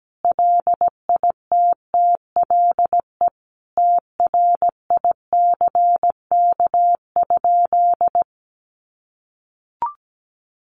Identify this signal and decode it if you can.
Morse code, heard as a keyed tone, about 17 wpm, 700 Hz. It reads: LITTLE TRICK?
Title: Sentences from Top 1500 Words 17wpm